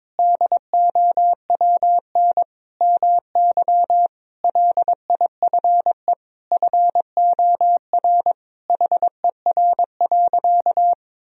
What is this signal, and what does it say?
Morse code, heard as a keyed tone, about 22 wpm, 700 Hz. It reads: DOWN MY LIFE FOR HER.
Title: Sentences from Top 1000 Words - Review Part 2 - 22wpm